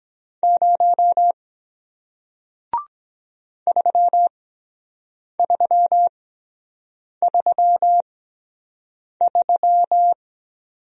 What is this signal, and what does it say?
Morse code, heard as a keyed tone, about 23 wpm, spaced as for 7 wpm, 700 Hz. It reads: 0 3333